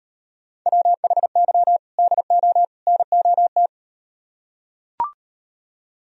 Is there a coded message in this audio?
Yes: Morse code from 0.7 to 3.7 s, heard as a keyed tone, about 38 wpm, 700 Hz. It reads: WHY DO NOT